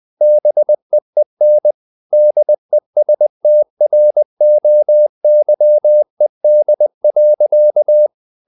Morse code, heard as a keyed tone, about 20 wpm, 600 Hz. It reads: BEEN DESTROYED.
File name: the old man and the sea part 2 20wpm